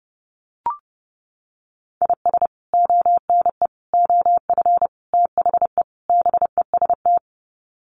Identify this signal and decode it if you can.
Morse code, heard as a keyed tone, about 30 wpm, 700 Hz. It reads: IS ONE OF THE BEST
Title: Sets of 5 Words 30wpm